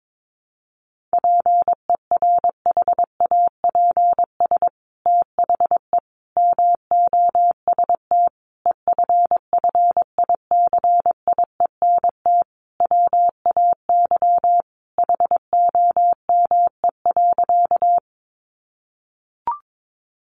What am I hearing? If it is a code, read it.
Morse code, heard as a keyed tone, about 22 wpm, 700 Hz. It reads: PERHAPS THE MOST EFFICIENT WAY HOME.
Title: Sentences from Top 1300 Words 22wpm